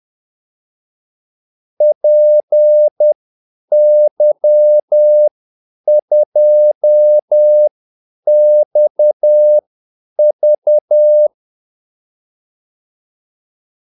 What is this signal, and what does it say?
Morse code, heard as a keyed tone, about 10 wpm, spaced as for 8 wpm, 600 Hz. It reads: PY2XV
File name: Callsigns-10-8wpm